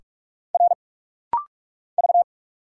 Morse code, heard as a keyed tone, about 46 wpm, 700 Hz. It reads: R V
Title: Single Letter-Number Alt - Rapid-Fire 45wpm